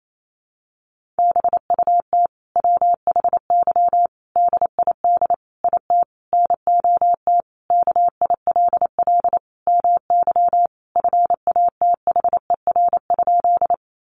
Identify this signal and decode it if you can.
Morse code, heard as a keyed tone, about 28 wpm, 700 Hz. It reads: BUT WHY DID IT NOT KILL MY FATHER?
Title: Sentences from Top 1400 Words 28wpm